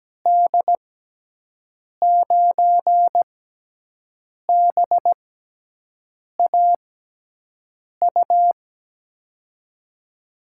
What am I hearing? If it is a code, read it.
Morse code, heard as a keyed tone, about 17 wpm, spaced as for 6 wpm, 700 Hz. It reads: D9BAU
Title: USA Call Signs - 6 Characters - 6x Character Spacing - 17wpm